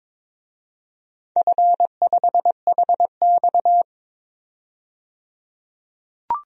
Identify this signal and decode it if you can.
Morse code, heard as a keyed tone, about 22 wpm, 700 Hz. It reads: F5HX
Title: DXCC-100 Call Signs - 4 Characters - 22wpm